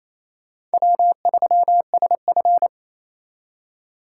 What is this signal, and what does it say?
Morse code, heard as a keyed tone, about 35 wpm, 700 Hz. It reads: W3SF
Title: Speed Racing - USA 1x2 Call Signs 28wpm